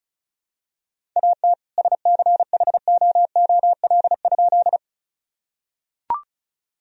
Morse code, heard as a keyed tone, about 36 wpm, 700 Hz. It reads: AT SCHOOL?
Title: Sentences from Top 1000 Words - Review Part 1 - 35wpm